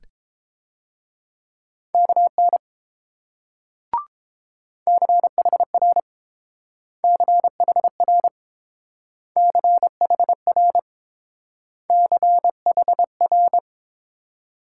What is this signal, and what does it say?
Morse code, heard as a keyed tone, about 31 wpm, 700 Hz. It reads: KN CHR CHR CHR CHR